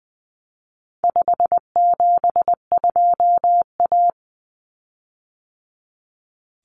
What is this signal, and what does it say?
Morse code, heard as a keyed tone, about 20 wpm, 700 Hz. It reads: 572A
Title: Common Cut Numbers - 4-Digit 20wpm